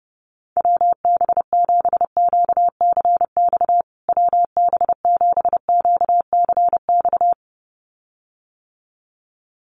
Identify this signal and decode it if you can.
Morse code, heard as a keyed tone, about 30 wpm, 700 Hz. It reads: WB7QCX WB7QCX
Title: USA Call Signs Repeated 2x 30wpm